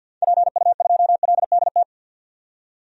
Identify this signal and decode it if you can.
Morse code, heard as a keyed tone, about 49 wpm, 700 Hz. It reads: PU2LDT
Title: DXCC-100 Call Signs - 6 Characters - 50wpm